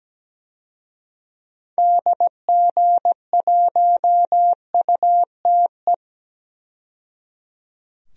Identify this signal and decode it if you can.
Morse code, heard as a keyed tone, about 17 wpm, 700 Hz. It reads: DG1UTE
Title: DXCC-100 Call Signs - 6 Characters - 17wpm